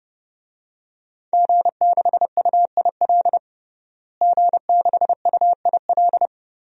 Morse code, heard as a keyed tone, about 32 wpm, 700 Hz. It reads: G6UIL G6UIL
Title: DXCC-100 Call Signs - 5 Characters - Repeated 3x - 30wpm